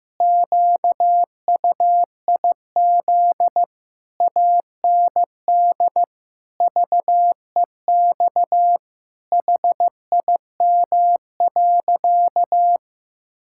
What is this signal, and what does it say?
Morse code, heard as a keyed tone, about 15 wpm, 700 Hz. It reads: QUIZ AND VEX HIM.
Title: Pangram Challenge v2 - 15wpm